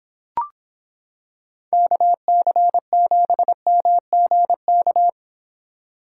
Morse code, heard as a keyed tone, about 26 wpm, 700 Hz. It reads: KC7MGK